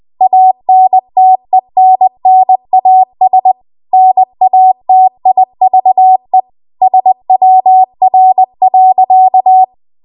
Morse code, heard as a keyed tone, about 20 wpm, 750 Hz. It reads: ANTENNAS NATIVE SWR.
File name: Archive / 20 WPM / 20WPM